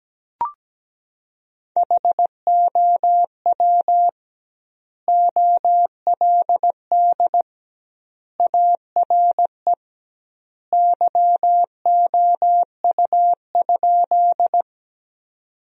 Morse code, heard as a keyed tone, about 17 wpm, 700 Hz. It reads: HOW OLD ARE YOU?
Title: 2x Word-Spacing - Sentences from Top 200 Words - 17wpm